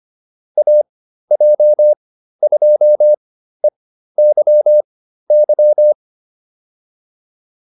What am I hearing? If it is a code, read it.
Morse code, heard as a keyed tone, about 25 wpm, spaced as for 13 wpm, 600 Hz. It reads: AJ2EYY
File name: QSO307 13